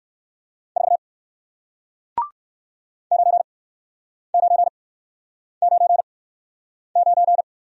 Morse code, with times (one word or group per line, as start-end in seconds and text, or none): 0.8-1.0 4
3.1-7.4 9999